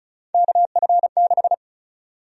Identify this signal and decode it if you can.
Morse code, heard as a keyed tone, about 36 wpm, 700 Hz. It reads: KF6
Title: USA Call Sign Prefixes - 35wpm